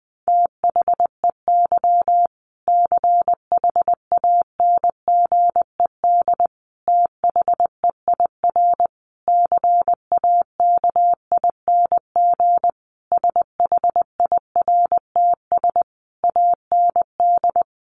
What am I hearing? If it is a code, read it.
Morse code, heard as a keyed tone, about 20 wpm, 700 Hz. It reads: THEY CHANGED THEIR CAKING SHIRTS AND
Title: A Sound Of Thunder 20wpm